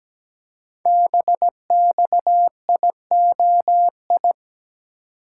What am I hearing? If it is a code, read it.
Morse code, heard as a keyed tone, about 17 wpm, 700 Hz. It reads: BXIOI
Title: Code Groups - 5 Letters - 17wpm